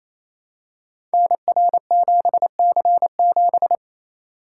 Morse code, heard as a keyed tone, about 27 wpm, 700 Hz. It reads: NR7C7
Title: Code Groups - 5 Letter:Numbers - 28wpm